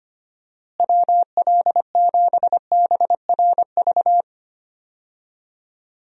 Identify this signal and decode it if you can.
Morse code, heard as a keyed tone, about 25 wpm, 700 Hz. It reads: WL7BRV
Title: USA Call Signs - 6 Characters - Repeated 2x - 25wpm